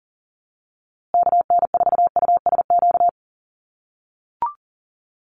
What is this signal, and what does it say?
Morse code, heard as a keyed tone, about 40 wpm, 700 Hz. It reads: KN4USQ
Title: USA Call Signs Repeated 3x 40wpm